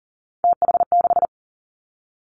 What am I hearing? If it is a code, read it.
Morse code, heard as a keyed tone, about 38 wpm, 700 Hz. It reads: TH6
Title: Three Challenging Letter-Number Combinations 40wpm